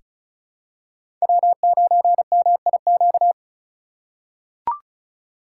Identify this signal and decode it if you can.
Morse code, heard as a keyed tone, about 34 wpm, 690 Hz. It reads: W9MIQ
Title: USA Call Signs - 5 Characters - 2x Character Spacing - 35wpm